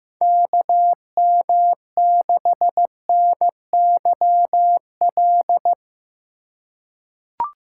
Morse code, heard as a keyed tone, about 15 wpm, 700 Hz. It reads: KM6NYL
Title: USA Call Signs - 6 Characters - 3x Character Spacing - 15wpm